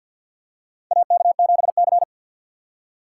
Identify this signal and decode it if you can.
Morse code, heard as a keyed tone, about 47 wpm, 700 Hz. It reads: AK7C
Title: USA Call Signs - 4 Characters - 2x Character Spacing - 50wpm